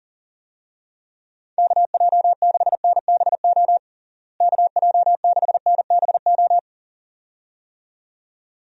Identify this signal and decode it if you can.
Morse code, heard as a keyed tone, about 40 wpm, 700 Hz. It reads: KJ6NBO KJ6NBO
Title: USA Call Signs - 6 Characters - Repeated 2x - 40wpm